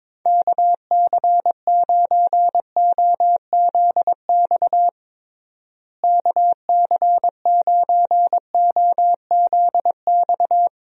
Morse code, heard as a keyed tone, about 22 wpm, 700 Hz. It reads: KC9OZX KC9OZX
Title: USA Call Signs - 6 Characters - Repeated 2x - 22wpm